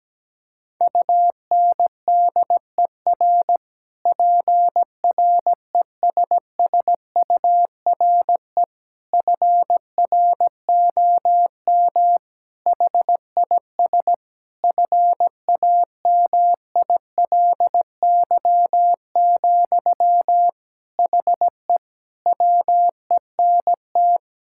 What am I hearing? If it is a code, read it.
Morse code, heard as a keyed tone, about 17 wpm, 700 Hz. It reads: UNDER PRESSURE FROM HIS FAMILY, HE WENT
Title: Sentences from Top 1000 Words - Review Part 6 - 17wpm